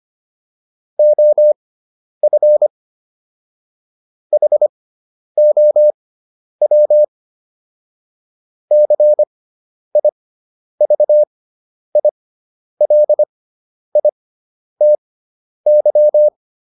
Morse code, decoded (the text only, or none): OF HOW CIVILITY